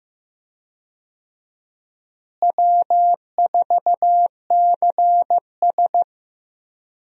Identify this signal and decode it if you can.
Morse code, heard as a keyed tone, about 15 wpm, 700 Hz. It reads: W4CS
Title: USA Call Signs - 4 Characters - Repeated 2x - 15wpm